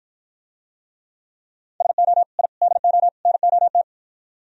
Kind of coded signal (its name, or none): Morse code